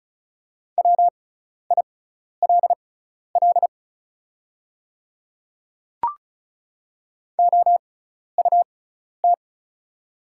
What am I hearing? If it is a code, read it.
Morse code, heard as a keyed tone, about 36 wpm, spaced as for 12 wpm, 700 Hz. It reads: WILL OUT
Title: 6x Character Spacing - Top 100 Words - 35wpm